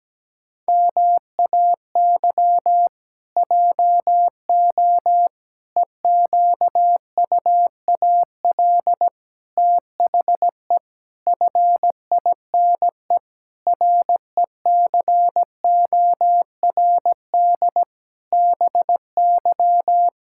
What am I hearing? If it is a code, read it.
Morse code, heard as a keyed tone, about 17 wpm, 700 Hz. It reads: MAY JO EQUAL THE FINE RECORD BY
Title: Pangram Challenge v2 - 17wpm